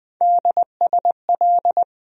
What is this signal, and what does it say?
Morse code, heard as a keyed tone, about 20 wpm, 700 Hz. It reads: DSL
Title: USA Call Sign Postfixes - 3 Letters 20wpm